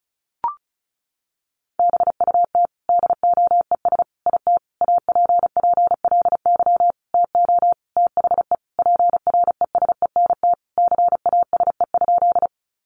Morse code, heard as a keyed tone, about 34 wpm, 700 Hz. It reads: BUT DOES IT APPLY TO THE PRESENT CASE?
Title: Sentences from Top 1200 Words 35wpm